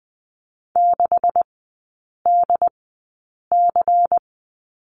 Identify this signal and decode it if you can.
Morse code, heard as a keyed tone, about 19 wpm, spaced as for 8 wpm, 700 Hz. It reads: 6DC
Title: 127 Lesson 21e - Number 6 - Three Characters 20wpm